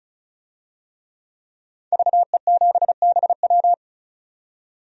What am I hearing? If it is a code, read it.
Morse code, heard as a keyed tone, about 36 wpm, 700 Hz. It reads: VE7BW
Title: DXCC-100 Call Signs - 5 Characters - 35wpm